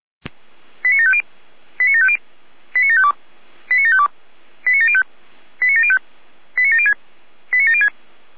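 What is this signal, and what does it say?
Five-tone selective call (ZVEI), heard as quick runs of five tones, tones of about 70 ms. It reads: 89750 89750 89752 89752 89705 89705 89706 89706
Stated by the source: Five-tone ZVEI selective call 89485, received on SDR.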